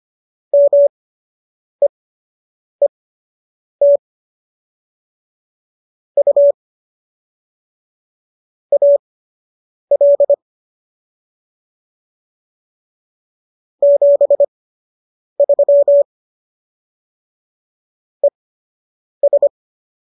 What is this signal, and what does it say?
Morse code, heard as a keyed tone, about 25 wpm, spaced as for 8 wpm, 570 Hz. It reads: MEET U AL 73 ES